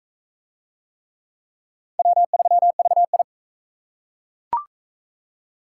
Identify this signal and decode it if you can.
Morse code, heard as a keyed tone, about 40 wpm, 700 Hz. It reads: W3VI